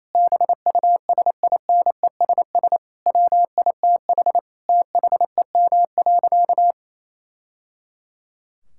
Morse code, heard as a keyed tone, about 28 wpm, 700 Hz. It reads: BUSINESS WITH THEM.
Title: Sentences from Top 1000 Words - Review Part 6 - 28wpm